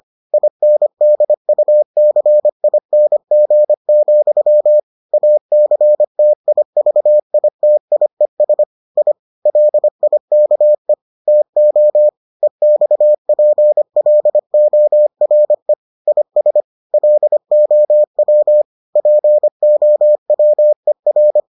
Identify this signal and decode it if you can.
Morse code, heard as a keyed tone, about 25 wpm, 600 Hz. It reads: INDUCING, ACTIVITIES I LIKE TO EXPLORE IS LOW POWER